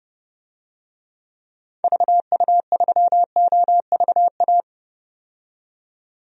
Morse code, heard as a keyed tone, about 30 wpm, 700 Hz. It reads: VU3OVA
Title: DXCC-100 Call Signs - 6 Characters - 30wpm